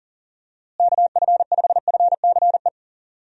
Morse code, heard as a keyed tone, about 37 wpm, 700 Hz. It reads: KF5FCE
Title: USA Call Signs - 6 Characters - Repeated 2x - 40wpm